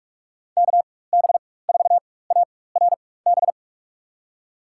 Morse code, heard as a keyed tone, about 42 wpm, spaced as for 20 wpm, 700 Hz. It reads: KB4ARB
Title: USA Call Signs - 4x Character Spacing - 45wpm